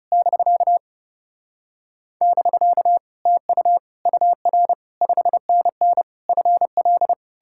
Morse code, heard as a keyed tone, about 40 wpm, 700 Hz. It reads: <BK> <BK> TU UR 5NN FL